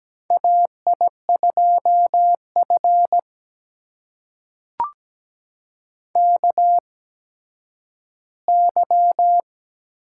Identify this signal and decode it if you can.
Morse code, heard as a keyed tone, about 17 wpm, spaced as for 10 wpm, 700 Hz. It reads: AI2F KY
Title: USA Call Signs - 4 Characters - 8x Character Spacing - 17wpm